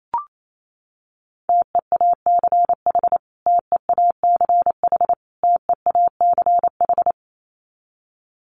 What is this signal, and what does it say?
Morse code, heard as a keyed tone, about 28 wpm, 700 Hz. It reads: TEACH TEACH TEACH
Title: Top 500 Words Repeated 28wpm